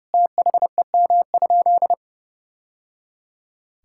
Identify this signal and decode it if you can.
Morse code, heard as a keyed tone, about 28 wpm, 700 Hz. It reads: THEM?
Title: Sentences from Top 1000 Words - Review Part 6 - 30wpm